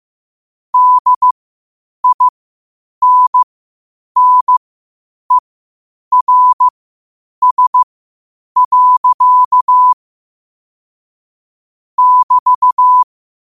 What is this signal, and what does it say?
Morse code, heard as a keyed tone, about 15 wpm, spaced as for 8 wpm, 1000 Hz. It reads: DINNERS. <BT>
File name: qotdfiles / 8wpm